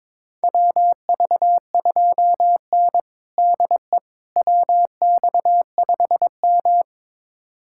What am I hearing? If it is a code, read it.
Morse code, heard as a keyed tone, about 22 wpm, 700 Hz. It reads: WV2N DE WX5M
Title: USA Callsign DE Callsign - 4 Characters - 22wpm